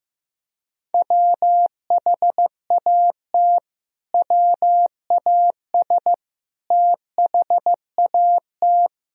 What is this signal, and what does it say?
Morse code, heard as a keyed tone, about 15 wpm, 700 Hz. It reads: WHAT WAS THAT
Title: Sentences from Top 1000 Words - Review Part 1 - 15wpm